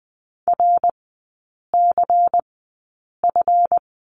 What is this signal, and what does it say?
Morse code, heard as a keyed tone, about 20 wpm, spaced as for 8 wpm, 700 Hz. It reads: RCF
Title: 138 Lesson 23e - Letter f - Three Characters 20wpm